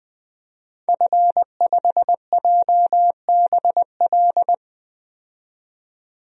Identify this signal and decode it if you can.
Morse code, heard as a keyed tone, about 20 wpm, 700 Hz. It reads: F5JBL
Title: DXCC-100 Call Signs - 5 Characters - 20wpm